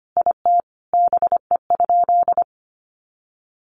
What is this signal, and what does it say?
Morse code, heard as a keyed tone, about 25 wpm, 700 Hz. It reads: IT BE?
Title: Sentences from Top 100 Words 25wpm